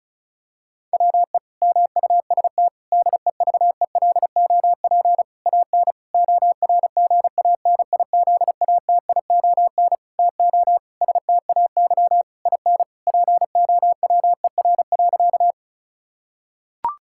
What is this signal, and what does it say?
Morse code, heard as a keyed tone, about 34 wpm, 700 Hz. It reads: WE MUST DEVELOP AN ORGANIZATION TO STAY IN POWER.
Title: Sentences from Top 1000 Words - Review Part 5 - 35wpm